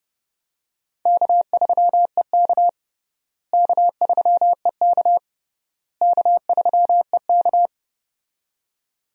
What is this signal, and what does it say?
Morse code, heard as a keyed tone, about 30 wpm, 700 Hz. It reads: K3EK K3EK K3EK
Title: USA Call Signs - 4 Characters - Repeated 3x - 30wpm